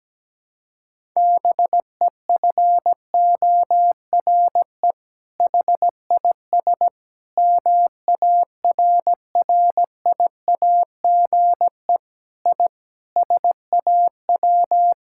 Morse code, heard as a keyed tone, about 17 wpm, 700 Hz. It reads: BEFORE HIS MARRIAGE I SAW
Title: Sentences from Top 1000 Words - Review Part 5 - 17wpm